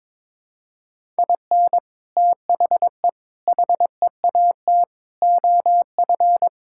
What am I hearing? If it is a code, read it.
Morse code, heard as a keyed tone, about 22 wpm, 700 Hz. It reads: IN THE HEAT OF
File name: practice / 22wpm / Sets of 6 Words - 22wpm